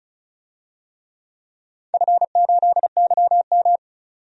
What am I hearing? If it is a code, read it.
Morse code, heard as a keyed tone, about 36 wpm, 700 Hz. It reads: F8YM
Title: DXCC-100 Call Signs - 4 Characters - 35wpm